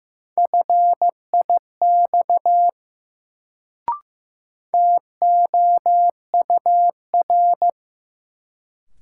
Morse code, heard as a keyed tone, about 15 wpm, 700 Hz. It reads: FIX TOUR
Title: Sentences from Top 1400 Words - New Word Drill - 15wpm